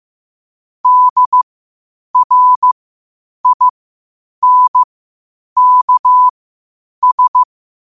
Morse code, heard as a keyed tone, about 15 wpm, spaced as for 8 wpm, 1000 Hz. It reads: DRINKS
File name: qotdfiles / 8wpm